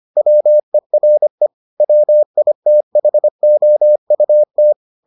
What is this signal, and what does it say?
Morse code, heard as a keyed tone, about 25 wpm, 600 Hz. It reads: WERE WITHOUT